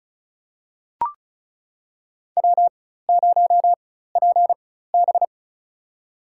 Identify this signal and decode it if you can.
Morse code, heard as a keyed tone, about 34 wpm, spaced as for 16 wpm, 700 Hz. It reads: W0PB
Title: USA Call Signs - 4x Character Spacing - 35wpm